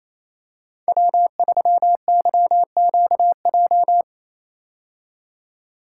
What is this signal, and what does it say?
Morse code, heard as a keyed tone, about 27 wpm, 700 Hz. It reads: W3YQJ
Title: Code Groups - 5 Letter:Numbers - 28wpm